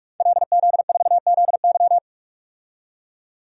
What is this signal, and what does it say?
Morse code, heard as a keyed tone, about 46 wpm, 700 Hz. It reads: LZ4ZY
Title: DXCC-100 Call Signs - 5 Characters - 45wpm